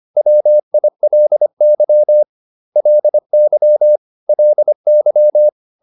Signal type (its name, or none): Morse code